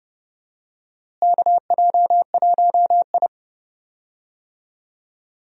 Morse code, heard as a keyed tone, about 32 wpm, 700 Hz. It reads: KJ1I
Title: USA Call Signs - 4 Characters - Repeated 3x - 30wpm